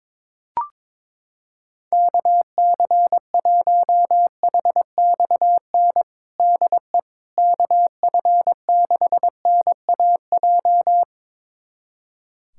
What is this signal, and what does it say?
Morse code, heard as a keyed tone, about 22 wpm, 700 Hz. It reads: KC1HXN DE KF6NAJ
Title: USA Callsign DE Callsign - 6 Characters - 22wpm